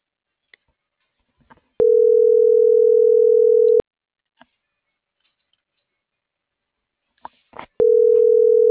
Ringback tone (North America), 440 + 480 Hz.